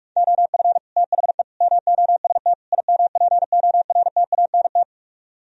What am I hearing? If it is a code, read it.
Morse code, heard as a keyed tone, about 44 wpm, 700 Hz. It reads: OF THE MOST IMPORTANT